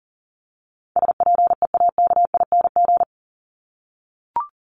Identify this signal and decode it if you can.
Morse code, heard as a keyed tone, about 40 wpm, 700 Hz. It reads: SPEAKING
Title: ING Suffix - Top 100 Words with Base Words 40wpm